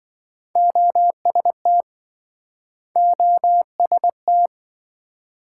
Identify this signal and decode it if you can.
Morse code, heard as a keyed tone, about 28 wpm, 700 Hz. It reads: OST OST